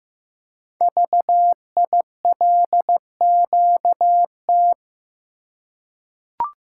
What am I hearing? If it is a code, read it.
Morse code, heard as a keyed tone, about 15 wpm, 700 Hz. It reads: VILQT
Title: Code Groups - 5 Letters - 15wpm